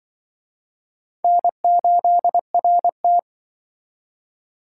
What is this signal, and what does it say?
Morse code, heard as a keyed tone, about 25 wpm, 700 Hz. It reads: N8RT
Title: Speed Racing - USA Call Signs 20wpm